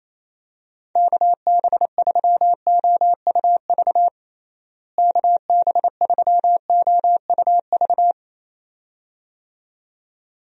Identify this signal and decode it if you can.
Morse code, heard as a keyed tone, about 28 wpm, 700 Hz. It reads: KB3OUV KB3OUV
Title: USA Call Signs - 6 Characters - Repeated 2x - 28wpm